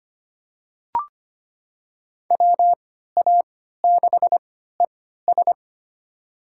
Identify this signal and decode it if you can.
Morse code, heard as a keyed tone, about 25 wpm, spaced as for 14 wpm, 700 Hz. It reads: WA6ES
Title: USA Call Signs - 5 Characters - 3x Character Spacing - 25wpm